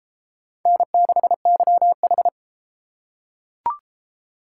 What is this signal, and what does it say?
Morse code, heard as a keyed tone, about 33 wpm, 700 Hz. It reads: N6YH